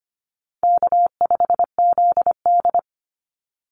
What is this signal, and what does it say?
Morse code, heard as a keyed tone, about 25 wpm, 700 Hz. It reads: K5ZD
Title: Easy Call Signs - 25wpm